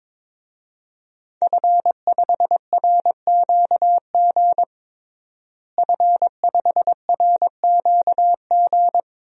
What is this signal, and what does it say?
Morse code, heard as a keyed tone, about 22 wpm, 700 Hz. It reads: F5RQG F5RQG
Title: DXCC-100 Call Signs - 5 Characters - Repeated 3x - 22wpm